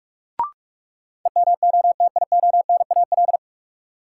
Morse code, heard as a keyed tone, about 46 wpm, 700 Hz. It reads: EMOTIONAL
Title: Sentences from Top 1000 Words - Review Part 5 - New Word Drill - 45wpm